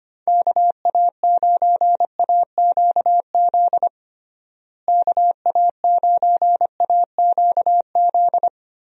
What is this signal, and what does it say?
Morse code, heard as a keyed tone, about 25 wpm, 700 Hz. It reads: KA9AQZ KA9AQZ
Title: USA Call Signs - 6 Characters - Repeated 3x - 25wpm